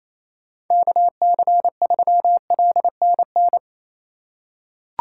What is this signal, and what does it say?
Morse code, heard as a keyed tone, about 28 wpm, 700 Hz. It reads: KC3LNN
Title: USA Call Signs - 2x Character Spacing - 28wpm